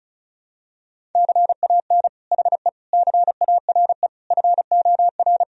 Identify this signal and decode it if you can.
Morse code, heard as a keyed tone, about 34 wpm, 700 Hz. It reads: CAN HE CARE FOR
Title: Sentences from Top 1000 Words - Review Part 1 - 35wpm